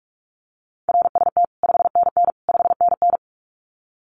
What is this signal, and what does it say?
Morse code, heard as a keyed tone, about 44 wpm, 700 Hz. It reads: RST 5NN 5NN
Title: Common QSO Phrases Encore 45wpm